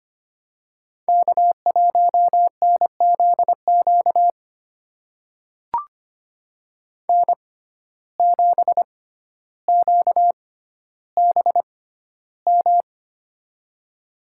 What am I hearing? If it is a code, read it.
Morse code, heard as a keyed tone, about 25 wpm, spaced as for 13 wpm, 700 Hz. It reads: K1NZQ N7QBM
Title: USA Call Signs - 5 Characters - 6x Character Spacing - 25wpm